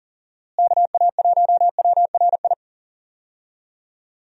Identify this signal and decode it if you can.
Morse code, heard as a keyed tone, about 40 wpm, 700 Hz. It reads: KA1WRI
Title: USA Call Signs - 6 Characters - Repeated 2x - 40wpm